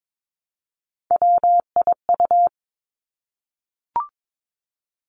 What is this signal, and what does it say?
Morse code, heard as a keyed tone, about 22 wpm, 700 Hz. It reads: WIU
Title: Three Challenging Letter-Number Combinations 22wpm